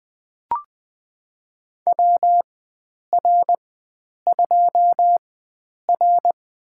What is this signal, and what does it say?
Morse code, heard as a keyed tone, about 20 wpm, spaced as for 9 wpm, 700 Hz. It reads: WR2R
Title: USA Call Signs - 4 Characters - 4x Character Spacing - 20wpm